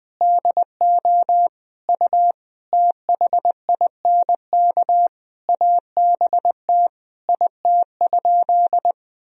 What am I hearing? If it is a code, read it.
Morse code, heard as a keyed tone, about 20 wpm, 700 Hz. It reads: DO U THINK ABT IT?
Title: Sentences from Top 200 Words and Abbreviations 20wpm